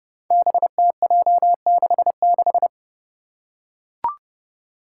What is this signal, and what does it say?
Morse code, heard as a keyed tone, about 30 wpm, 700 Hz. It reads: BTJ66
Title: Five Challenging Letter-Number Combinations 30wpm